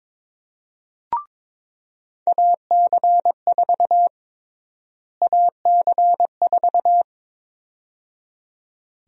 Morse code, heard as a keyed tone, about 21 wpm, 700 Hz. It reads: AC4 AC4
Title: USA Call Sign Prefixes - Repeated 2x - 22wpm